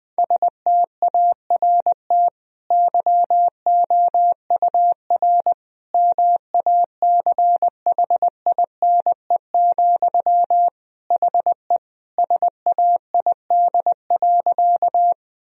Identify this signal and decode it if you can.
Morse code, heard as a keyed tone, about 20 wpm, 700 Hz. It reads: START YOUR MACHINE, HE SAID.
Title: Sentences from Top 1000 Words - Review Part 2 - 20wpm